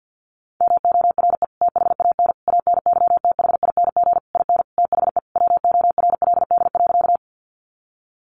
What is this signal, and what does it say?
Morse code, heard as a keyed tone, about 48 wpm, 700 Hz. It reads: MORE THAN ANYTHING IN THE WORLD.
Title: Sentences from Top 1000 Words 50wpm